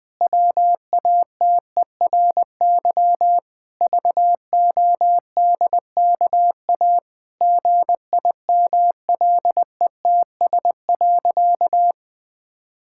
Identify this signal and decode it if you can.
Morse code, heard as a keyed tone, about 21 wpm, 700 Hz. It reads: WATERY VODKA GIMLETS.
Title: Pangram Challenge v2 - 20wpm